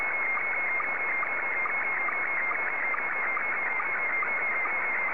SSTV (slow-scan television), Scottie 1.